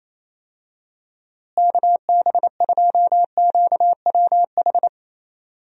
Morse code, heard as a keyed tone, about 28 wpm, 700 Hz. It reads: KB2QWH
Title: USA Call Signs - 6 Characters - Repeated 3x - 28wpm